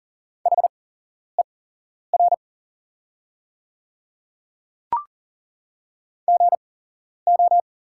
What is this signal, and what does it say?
Morse code, heard as a keyed tone, about 39 wpm, spaced as for 11 wpm, 700 Hz. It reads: HER GO
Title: 8x Character Spacing - Top 100 Words - 40wpm